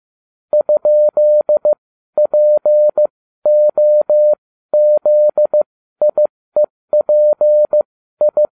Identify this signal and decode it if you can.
Morse code, heard as a keyed tone, about 15 wpm, spaced as for 12 wpm, 600 Hz. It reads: ?POZIEPI